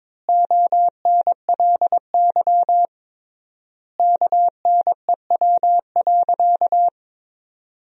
Morse code, heard as a keyed tone, about 22 wpm, 700 Hz. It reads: ONLY KNEW.
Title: 3x Word-Spacing - Sentences from Top 500 Words - 22wpm